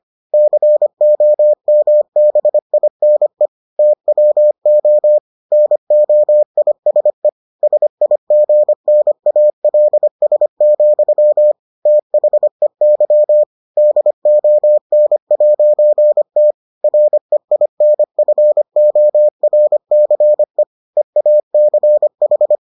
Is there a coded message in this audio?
Yes: Morse code from 0.3 to 22.6 s, heard as a keyed tone, about 25 wpm, 600 Hz. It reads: COMBINE TWO NOISE SIGNALS, THEY DON'T REINFORCE EACH